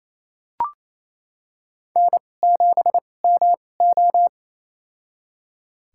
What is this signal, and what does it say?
Morse code, heard as a keyed tone, about 27 wpm, spaced as for 20 wpm, 700 Hz. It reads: N7MO
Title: USA Call Signs - 4 Characters - 2x Character Spacing - 28wpm